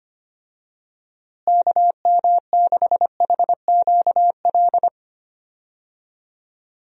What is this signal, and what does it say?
Morse code, heard as a keyed tone, about 25 wpm, 700 Hz. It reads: KM6HQL
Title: USA Call Signs - 6 Characters - Repeated 3x - 25wpm